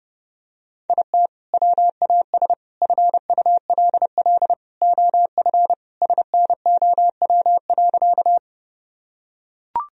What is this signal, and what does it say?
Morse code, heard as a keyed tone, about 30 wpm, 700 Hz. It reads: IT WAS FULL OF SNOW.
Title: Sentences from Top 1000 Words - Review Part 3 - 30wpm